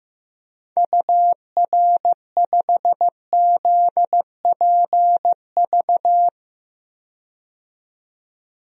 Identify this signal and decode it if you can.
Morse code, heard as a keyed tone, about 15 wpm, 700 Hz. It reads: UR5ZPV
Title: DXCC-100 Call Signs - 6 Characters - 15wpm